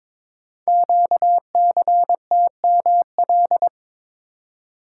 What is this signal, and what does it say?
Morse code, heard as a keyed tone, about 22 wpm, 700 Hz. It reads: QCTML
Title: Code Groups - 5 Letters - 22wpm